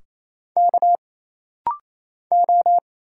Morse code, heard as a keyed tone, about 28 wpm, 700 Hz. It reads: K O
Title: Single Letters - Rapid-Fire 28wpm